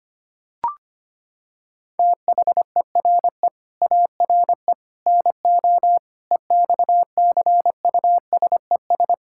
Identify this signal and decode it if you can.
Morse code, heard as a keyed tone, about 25 wpm, 700 Hz. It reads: THERE ARE NO EXCUSES
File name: practice / 25wpm / Sentences with 3-Word Phrases - 25wpm